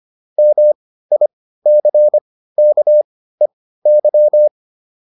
Morse code, heard as a keyed tone, about 25 wpm, spaced as for 15 wpm, 600 Hz. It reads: MICKEY